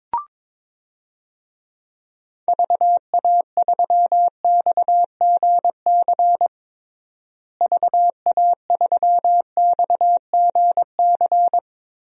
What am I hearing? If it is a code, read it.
Morse code, heard as a keyed tone, about 22 wpm, 700 Hz. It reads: VA3XGC VA3XGC
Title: DXCC-100 Call Signs - 6 Characters - Repeated 3x - 22wpm